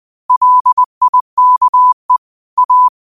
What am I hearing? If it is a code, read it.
Morse code, heard as a keyed tone, about 20 wpm, 1000 Hz. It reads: LIKE A
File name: qotdfiles / 20wpm